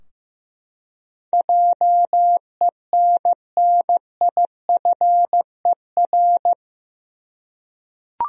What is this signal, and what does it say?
Morse code, heard as a keyed tone, about 15 wpm, 700 Hz. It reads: JENNIFER
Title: USA Common Names - 15wpm